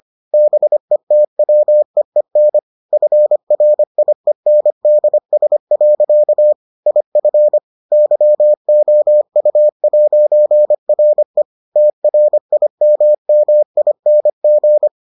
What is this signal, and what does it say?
Morse code, heard as a keyed tone, about 25 wpm, 600 Hz. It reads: BETWEEN FRIENDS. IF YOU'RE TRIMMING